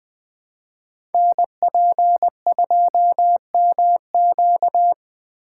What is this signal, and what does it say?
Morse code, heard as a keyed tone, about 20 wpm, 700 Hz. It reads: NP2MQ
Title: Code Groups - 5 Letter:Numbers - 20wpm